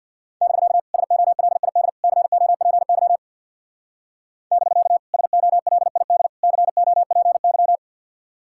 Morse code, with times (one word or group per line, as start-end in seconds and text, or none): 0.4-0.8 <BK>
0.9-1.9 SOLID
2.0-3.2 COPY
4.5-5.0 <BK>
5.1-6.3 SOLID
6.4-7.8 COPY